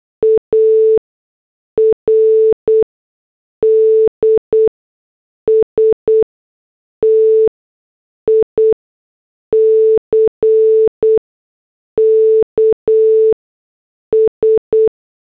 Morse code, heard as a keyed tone, about 8 wpm, spaced as for 6 wpm, 440 Hz. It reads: ARDSTICKS
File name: morse / headlines